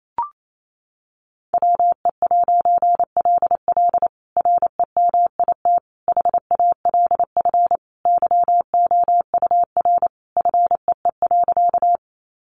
Morse code, heard as a keyed tone, about 28 wpm, 700 Hz. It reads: WE'LL REMIT HALF YOUR FEE.
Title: A Sound Of Thunder 28wpm